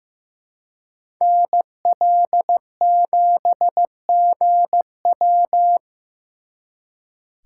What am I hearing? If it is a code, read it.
Morse code, heard as a keyed tone, about 18 wpm, 700 Hz. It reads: NL7GW
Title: Speed Racing - USA 2x2 Call Signs 15wpm